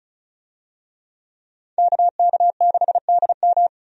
Morse code, heard as a keyed tone, about 36 wpm, 700 Hz. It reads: KK6DM
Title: USA Call Signs - 5 Characters - 35wpm